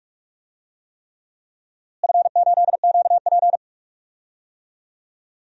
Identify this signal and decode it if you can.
Morse code, heard as a keyed tone, about 45 wpm, 700 Hz. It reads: F8QP
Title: DXCC-100 Call Signs - 4 Characters - Repeated 3x - 45wpm